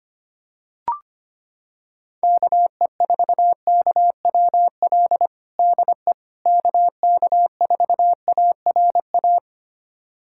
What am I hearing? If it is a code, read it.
Morse code, heard as a keyed tone, about 25 wpm, 700 Hz. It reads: KE4KWL DE KK4ARA
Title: USA Callsign DE Callsign - 6 Characters - 25wpm